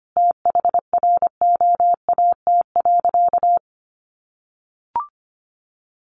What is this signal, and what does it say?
Morse code, heard as a keyed tone, about 25 wpm, 700 Hz. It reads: THROAT.
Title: A Sound Of Thunder 25wpm